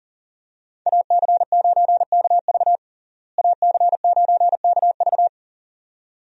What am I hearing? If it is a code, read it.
Morse code, heard as a keyed tone, about 37 wpm, 700 Hz. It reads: AC9KV AC9KV
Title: USA Call Signs - 5 Characters - Repeated 2x - 40wpm